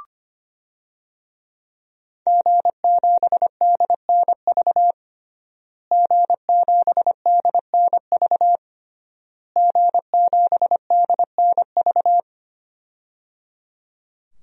Morse code, heard as a keyed tone, about 25 wpm, 700 Hz. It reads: G7DNV G7DNV G7DNV
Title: DXCC-100 Call Signs - 5 Characters - Repeated 3x - 25wpm